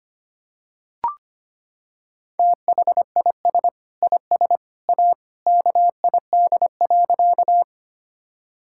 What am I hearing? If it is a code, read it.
Morse code, heard as a keyed tone, about 25 wpm, 700 Hz. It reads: THIS IS A KID.
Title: Sentences from Top 1600 Words 25wpm